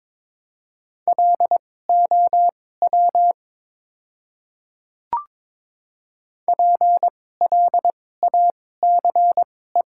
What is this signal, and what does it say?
Morse code, heard as a keyed tone, about 22 wpm, spaced as for 16 wpm, 700 Hz. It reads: LOW PLACE
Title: 2x Character Spacing - Top 200 Words - 22wpm